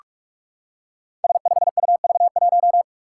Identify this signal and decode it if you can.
Morse code, heard as a keyed tone, about 44 wpm, 700 Hz. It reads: S5UV1
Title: Five Challenging Letter-Number Combinations 45wpm